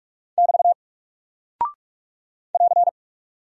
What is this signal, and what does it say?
Morse code, heard as a keyed tone, about 44 wpm, 700 Hz. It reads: <BT> <AR>
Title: Single Character (Letter:Number:Punctuation:Prosign) Alt 45wpm